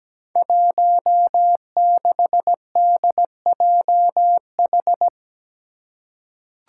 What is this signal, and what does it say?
Morse code, heard as a keyed tone, about 17 wpm, 700 Hz. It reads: 16DJH
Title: Five Challenging Letter-Number Combinations 17wpm